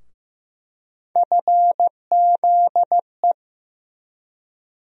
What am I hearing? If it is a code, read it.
Morse code, heard as a keyed tone, about 15 wpm, 700 Hz. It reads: FZE
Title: USA Call Sign Postfixes - 3 Letters 15wpm